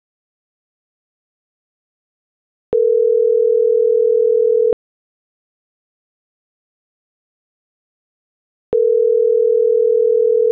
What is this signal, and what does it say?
Ringback tone (North America), 440 + 480 Hz.